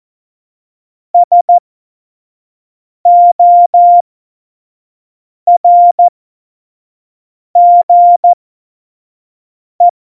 Morse code, heard as a keyed tone, about 14 wpm, spaced as for 5 wpm, 700 Hz. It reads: SORGE
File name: Morsekurs / Witze / Witz 26. Dezember 2017, 25 F70